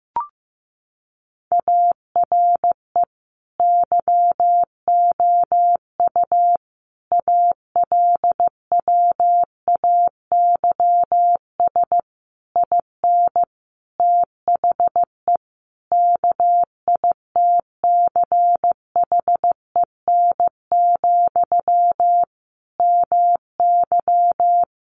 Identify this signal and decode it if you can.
Morse code, heard as a keyed tone, about 15 wpm, 700 Hz. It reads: ARE YOU ALWAYS IN THE KITCHEN, MY
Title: Sentences from Top 1400 Words 15wpm